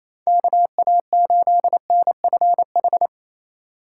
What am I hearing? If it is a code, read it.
Morse code, heard as a keyed tone, about 28 wpm, 700 Hz. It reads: KA8NFH
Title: USA Call Signs - 6 Characters - 2x Character Spacing - 28wpm